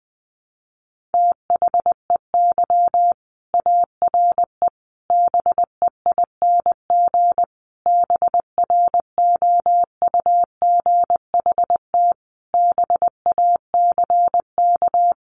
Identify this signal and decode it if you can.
Morse code, heard as a keyed tone, about 19 wpm, 700 Hz. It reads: THEY ARE BEING BROUGHT BACK
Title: Sentences from Top 600 Words 20wpm